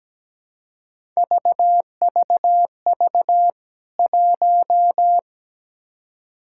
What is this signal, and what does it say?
Morse code, heard as a keyed tone, about 17 wpm, 700 Hz. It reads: VVV 1
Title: K1USN SST QSO Practice 17wpm